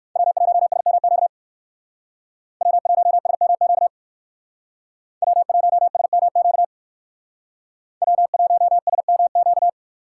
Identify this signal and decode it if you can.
Morse code, heard as a keyed tone, about 58 wpm, 700 Hz. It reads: W1SMQ W1SMQ W1SMQ W1SMQ